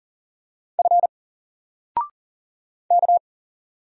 Morse code, heard as a keyed tone, about 39 wpm, 700 Hz. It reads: F K
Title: Single Letter-Number 40wpm